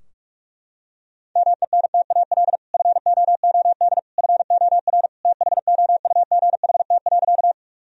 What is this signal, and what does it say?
Morse code, heard as a keyed tone, about 41 wpm, 700 Hz. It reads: MENTAL FOOD FOR THOUGHT.
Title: Sentences from Top 1000 Words - Review Part 4 - 45wpm